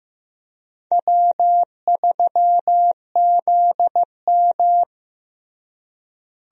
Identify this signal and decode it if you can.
Morse code, heard as a keyed tone, about 15 wpm, 700 Hz. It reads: W3ZM
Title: USA Call Signs - 4 Characters - Repeated 3x - 15wpm